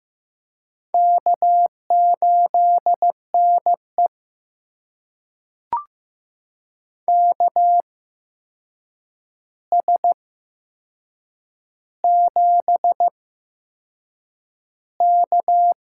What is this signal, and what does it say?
Morse code, heard as a keyed tone, about 15 wpm, spaced as for 6 wpm, 700 Hz. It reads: K8NE KS7K
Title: USA Call Signs - 4 Characters - 8x Character Spacing - 15wpm